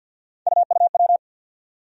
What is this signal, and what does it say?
Morse code, heard as a keyed tone, about 48 wpm, 700 Hz. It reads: UUW
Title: USA Call Sign Postfixes - 3 Letters 50wpm